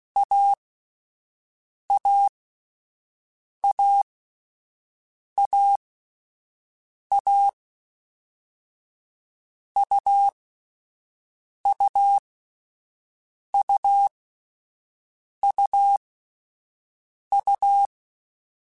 Morse code, heard as a keyed tone, about 16 wpm, spaced as for 5 wpm, 800 Hz. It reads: AAAAA UUUUU
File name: Difficult Combos / Difficult Combos-mp3s / AUV AUV4 @5